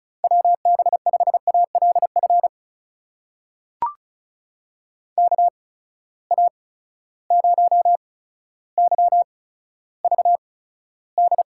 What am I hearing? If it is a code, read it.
Morse code, heard as a keyed tone, about 36 wpm, spaced as for 15 wpm, 700 Hz. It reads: WB5ALF KA0YVD
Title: USA Call Signs - 6 Characters - 8x Character Spacing - 35wpm